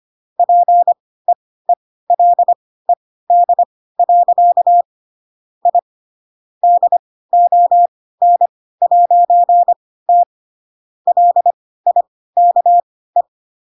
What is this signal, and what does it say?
Morse code, heard as a keyed tone, about 25 wpm, spaced as for 16 wpm, 700 Hz. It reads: PEELED. I DON'T LIKE
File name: Pinocchio / Pinocchio ch7 10 25 ik1hge